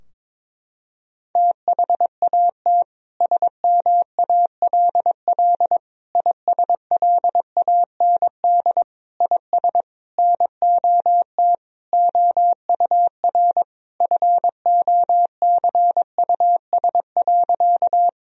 Morse code, heard as a keyed tone, about 22 wpm, 700 Hz. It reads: THAT SMALL ISLAND IS NOT OUR FOCUS.
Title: Sentences from Top 1000 Words - Review Part 4 - 22wpm